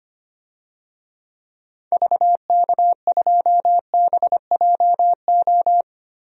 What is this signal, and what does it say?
Morse code, heard as a keyed tone, about 25 wpm, 700 Hz. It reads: VK2BJO
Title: DXCC-100 Call Signs - 6 Characters - Repeated 2x - 25wpm